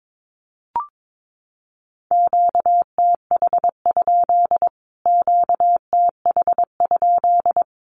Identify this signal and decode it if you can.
Morse code, heard as a keyed tone, about 22 wpm, 700 Hz. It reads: QTH? QTH?
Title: Q-Codes Repeated 3x 22wpm